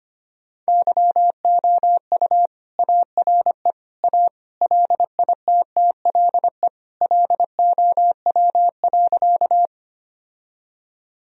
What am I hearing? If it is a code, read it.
Morse code, heard as a keyed tone, about 25 wpm, 700 Hz. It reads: YOU ARE A LITTLE LOW.
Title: Sentences from Top 1000 Words - Review Part 1 - 25wpm